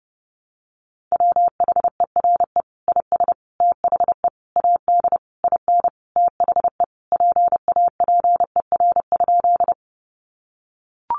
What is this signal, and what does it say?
Morse code, heard as a keyed tone, about 30 wpm, 700 Hz. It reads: WHERE IS THE AD IN THE PAPER?
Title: Sentences from Top 1300 Words 30wpm